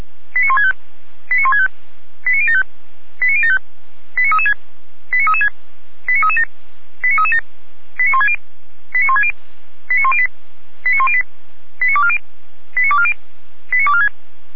Five-tone selective call (ZVEI), heard as quick runs of five tones, tones of about 80 ms. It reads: 89165 89165 89075 89075 89336 89336 89337 89337 89160 89160 89108 89108 89240 89240 89246